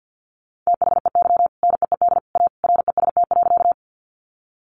Morse code, heard as a keyed tone, about 50 wpm, 700 Hz. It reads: THEY NEED A REST.
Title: Sentences from Top 500 Words 50wpm